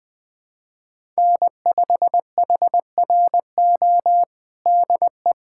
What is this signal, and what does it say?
Morse code, heard as a keyed tone, about 21 wpm, 700 Hz. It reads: N5HRO DE
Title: USA Callsign DE Callsign 20wpm